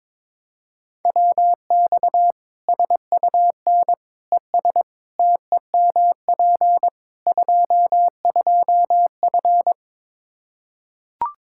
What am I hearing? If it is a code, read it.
Morse code, heard as a keyed tone, about 22 wpm, 700 Hz. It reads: WX SUN ES TEMP 22F
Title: Practice QSO - Exchange 2 - Info - 22wpm